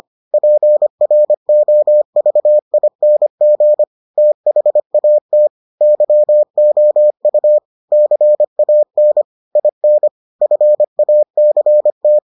Morse code, heard as a keyed tone, about 25 wpm, 600 Hz. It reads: PROVING THAT YOU CAN IN FACT